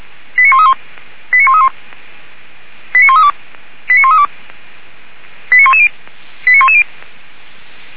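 Five-tone selective call (ZVEI), heard as quick runs of five tones, tones of about 70 ms. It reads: 89131 89131 89132 89132 89110 89110
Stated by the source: Fire brigade ZVEI selective call received by SDR, Feuerwehr Orlamünde.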